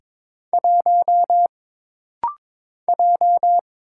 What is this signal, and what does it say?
Morse code, heard as a keyed tone, about 22 wpm, 700 Hz. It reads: 1 J
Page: This rapid-fire timing is geared to push your ICR (Instant Character Recognition) skills.